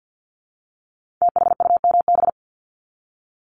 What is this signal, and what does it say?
Morse code, heard as a keyed tone, about 50 wpm, 700 Hz. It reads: THUMB
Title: Multi-letter Phonemes Repeated 3x 50wpm